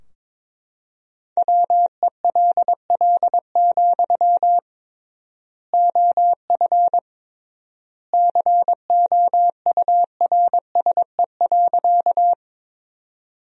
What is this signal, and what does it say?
Morse code, heard as a keyed tone, about 22 wpm, 700 Hz. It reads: WELL, OF COURSE.
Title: 3x Word-Spacing - Sentences from Top 500 Words - 22wpm